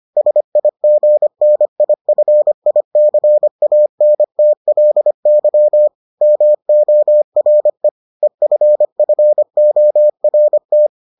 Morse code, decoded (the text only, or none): SIGNIFICANTLY MORE EFFORT